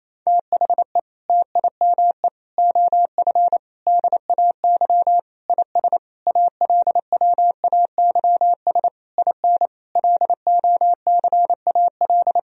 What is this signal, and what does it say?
Morse code, heard as a keyed tone, about 28 wpm, 700 Hz. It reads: THE TIME OF DAY IS ALWAYS IN LOCAL
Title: Sentences from Top 1000 Words - Review Part 3 - 28wpm